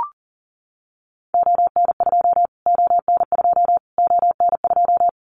Morse code, heard as a keyed tone, about 40 wpm, 700 Hz. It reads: ON2 ON2 ON2
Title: Callsign Prefixes for Top 25 DXCC - Repeated 3x - 40wpm